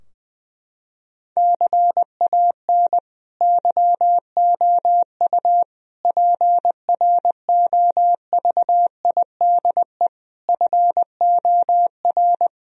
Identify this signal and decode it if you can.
Morse code, heard as a keyed tone, about 20 wpm, 700 Hz. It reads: CAN YOU PROVIDE FOR
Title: Sentences from Top 1000 Words - Review Part 2 - 20wpm